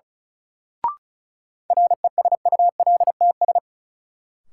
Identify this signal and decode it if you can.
Morse code, heard as a keyed tone, about 34 wpm, 700 Hz. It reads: RESULTS
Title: Sets of 4 Words - New Word Drill - 35wpm